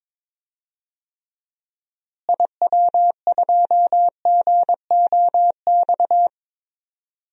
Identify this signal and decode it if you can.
Morse code, heard as a keyed tone, about 22 wpm, 700 Hz. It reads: IW2GOX
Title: DXCC-100 Call Signs - 6 Characters - Repeated 2x - 22wpm